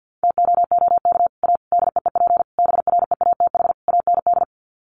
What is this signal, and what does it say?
Morse code, heard as a keyed tone, about 49 wpm, 700 Hz. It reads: TOOK A DEEP BREATH AND
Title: Sets of 5 Words Encore 50wpm